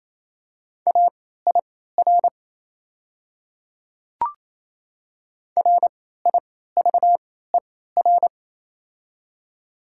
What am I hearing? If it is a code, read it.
Morse code, heard as a keyed tone, about 28 wpm, spaced as for 16 wpm, 700 Hz. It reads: AIR RIVER
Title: 3x Character Spacing - Top 300 Words - 28wpm